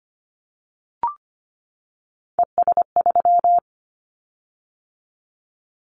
Morse code, heard as a keyed tone, about 25 wpm, 700 Hz. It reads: ES3
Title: Callsign Prefixes for Top 50 DXCC - 25wpm